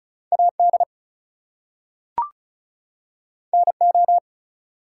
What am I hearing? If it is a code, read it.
Morse code, heard as a keyed tone, about 36 wpm, 700 Hz. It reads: AD NO
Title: Two Letter Words - 35wpm